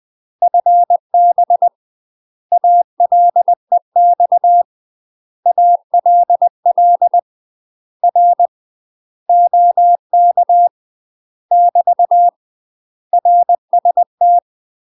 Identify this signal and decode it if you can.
Morse code, heard as a keyed tone, about 20 wpm, 700 Hz. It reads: FB ALEX ALL R OK <BT> RST
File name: UK MASTER QSOs TRAINING / 20WPM / UK MASTER QSOs 0065